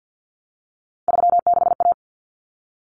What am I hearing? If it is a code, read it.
Morse code, heard as a keyed tone, about 50 wpm, 700 Hz. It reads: 36A
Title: Common Cut Numbers - 3-Digit 50wpm